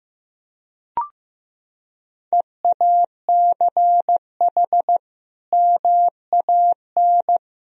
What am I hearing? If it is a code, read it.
Morse code, heard as a keyed tone, about 15 wpm, 700 Hz. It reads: EACH MAN
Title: Sentences from Top 1000 Words - Review Part 2 - 15wpm